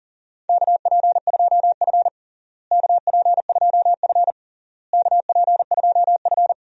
Morse code, heard as a keyed tone, about 37 wpm, 700 Hz. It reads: KP2F KP2F KP2F
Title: DXCC-100 Call Signs - 4 Characters - Repeated 3x - 40wpm